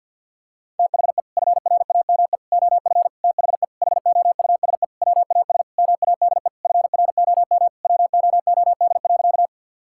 Morse code, heard as a keyed tone, about 48 wpm, 700 Hz. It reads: THE FRAME OF THE HOUSE WAS MADE FROM WOOD.
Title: Sentences from Top 1000 Words - Review Part 6 - 50wpm